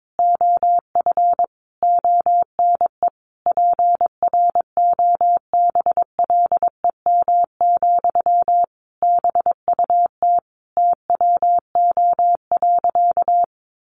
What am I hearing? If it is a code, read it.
Morse code, heard as a keyed tone, about 22 wpm, 700 Hz. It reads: OF ONE PROBLEM, BUT TWO.
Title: Sentences from Top 1400 Words 22wpm